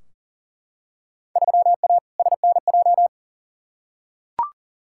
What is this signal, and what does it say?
Morse code, heard as a keyed tone, about 40 wpm, 700 Hz. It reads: 3A SNJ
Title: Field Day Exchanges - 40wpm